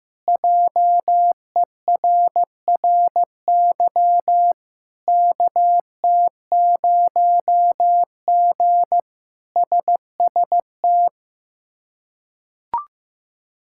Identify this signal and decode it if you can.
Morse code, heard as a keyed tone, about 15 wpm, 700 Hz. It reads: JERRY KT0G SST
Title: K1USN SST QSO Practice - 15wpm